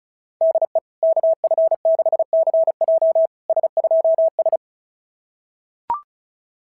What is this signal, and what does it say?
Morse code, heard as a keyed tone, about 36 wpm, 690 Hz. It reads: DE KF6CJ S2S
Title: SOTA QSO Practice 35wpm